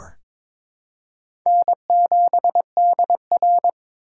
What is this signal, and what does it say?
Morse code, heard as a keyed tone, about 22 wpm, 700 Hz. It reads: N7DR
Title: K1USN SST QSO Practice - 22wpm